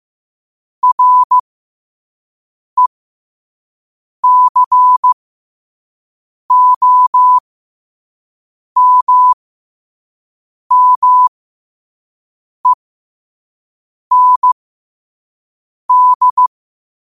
Morse code, heard as a keyed tone, about 15 wpm, spaced as for 5 wpm, 1000 Hz. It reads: RECOMMEND
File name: qotdfiles / 5wpm